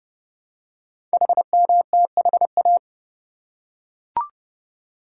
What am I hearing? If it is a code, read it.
Morse code, heard as a keyed tone, about 29 wpm, 700 Hz. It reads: HMTHA
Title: Code Groups - 5 Letters - 30wpm